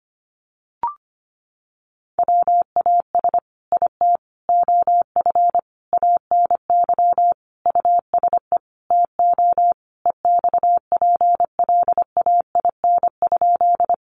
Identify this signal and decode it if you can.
Morse code, heard as a keyed tone, about 25 wpm, 700 Hz. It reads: WAS IT OF ANY USE TO EXPLAIN?
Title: Sentences from Top 1100 Words 25wpm